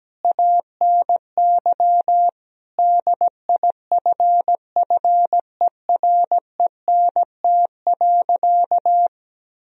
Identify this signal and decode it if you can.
Morse code, heard as a keyed tone, about 17 wpm, 700 Hz. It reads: ANY DIFFERENT.
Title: Sentences from Top 1000 Words - Review Part 3 - 17wpm